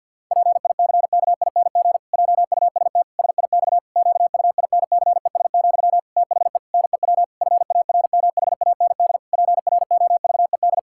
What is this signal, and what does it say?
Morse code, heard as a keyed tone, about 50 wpm, 700 Hz. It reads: PICKING JUST SIX QUINCES, THE NEW FARMHAND PROVED
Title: Pangram Challenge v2 - 50wpm